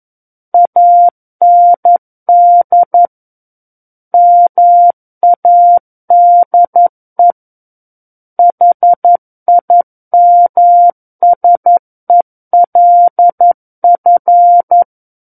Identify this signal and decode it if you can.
Morse code, heard as a keyed tone, about 11 wpm, 700 Hz. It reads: AND MADE HIMSELF
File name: Book One / 02-Chapter 2 - 11 WPM